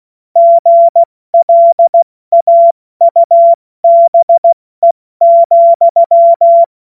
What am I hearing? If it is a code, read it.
Morse code, heard as a keyed tone, about 16 wpm, 680 Hz. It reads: GLAUBE,